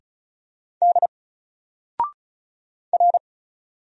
Morse code, heard as a keyed tone, about 36 wpm, 700 Hz. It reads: D R
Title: Single Character (Letter:Number:Punctuation:Prosign) 35wpm